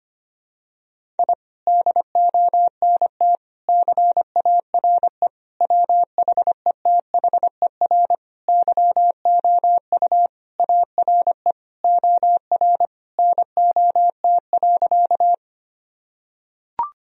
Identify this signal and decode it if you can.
Morse code, heard as a keyed tone, about 25 wpm, 700 Hz. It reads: I DONT CARE WHETHER YOU ARE OR NOT.
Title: Sentences from Top 1000 Words - Review Part 3 - 25wpm